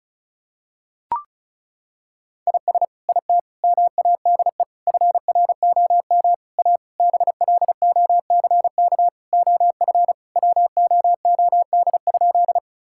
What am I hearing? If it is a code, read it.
Morse code, heard as a keyed tone, about 36 wpm, 700 Hz. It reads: IS IT MADE FROM A BLOCK OF WOOD?
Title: Sentences from Top 1600 Words 35wpm